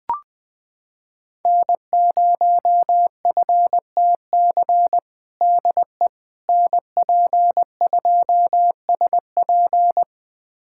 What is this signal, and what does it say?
Morse code, heard as a keyed tone, about 20 wpm, 700 Hz. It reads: N0FTC DE NP2SP
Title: USA Callsign DE Callsign - 5 Characters - 20wpm